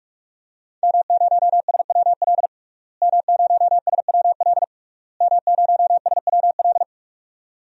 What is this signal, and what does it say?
Morse code, heard as a keyed tone, about 41 wpm, 700 Hz. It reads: M0SWL M0SWL M0SWL
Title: DXCC-100 Call Signs - 5 Characters - Repeated 3x - 45wpm